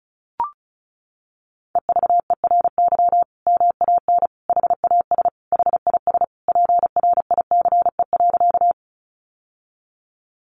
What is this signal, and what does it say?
Morse code, heard as a keyed tone, about 34 wpm, 700 Hz. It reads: EVERY MAN HAS HIS PRICE.
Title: Sentences from Top 600 Words 35wpm